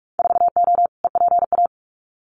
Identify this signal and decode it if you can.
Morse code, heard as a keyed tone, about 46 wpm, 700 Hz. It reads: 4O EPA
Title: Winter Field Day Exchanges 45wpm